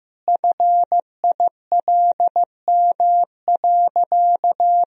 Morse code, heard as a keyed tone, about 15 wpm, 700 Hz. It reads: FILM.
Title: Sentences from Top 1000 Words - Review Part 3 - 15wpm